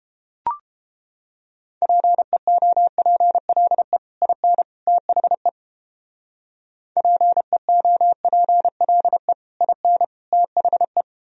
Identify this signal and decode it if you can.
Morse code, heard as a keyed tone, about 29 wpm, 700 Hz. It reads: PEOPLE IN THE PEOPLE IN THE